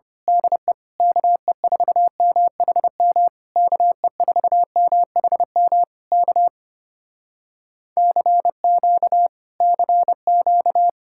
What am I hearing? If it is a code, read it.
Morse code, heard as a keyed tone, about 25 wpm, 700 Hz. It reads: DE KE4MHM KE4MHM K CQ CQ
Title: Speed-Racing Common QSO Phrases Encore - 25wpm